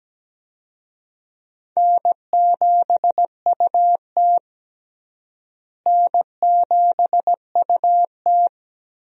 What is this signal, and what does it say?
Morse code, heard as a keyed tone, about 17 wpm, 700 Hz. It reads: N7UT N7UT
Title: USA Call Signs - 4 Characters - Repeated 2x - 17wpm